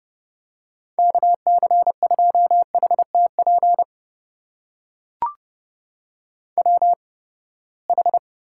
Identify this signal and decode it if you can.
Morse code, heard as a keyed tone, about 32 wpm, spaced as for 21 wpm, 700 Hz. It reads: KC2HTP WH
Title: USA Call Signs - 6 Characters - 8x Character Spacing - 30wpm